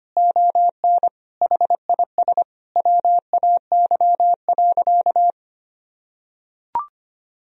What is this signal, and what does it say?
Morse code, heard as a keyed tone, about 25 wpm, 700 Hz. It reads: ON HIS WAY.
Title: Sentences from Top 300 Words and Abbreviations 25wpm